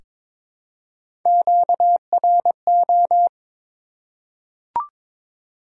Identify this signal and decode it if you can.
Morse code, heard as a keyed tone, about 21 wpm, 700 Hz. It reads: QRO
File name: Top 90 QSO Elements - 6x Character Spacing - 22wpm